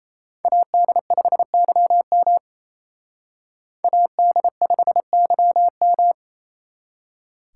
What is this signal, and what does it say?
Morse code, heard as a keyed tone, about 35 wpm, 700 Hz. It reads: AD5YM AD5YM